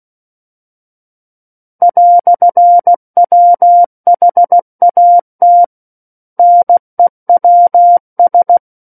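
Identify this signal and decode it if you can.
Morse code, heard as a keyed tone, about 16 wpm, 700 Hz. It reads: "WHAT NEWS
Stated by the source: Morse practice of book text at 16 wpm.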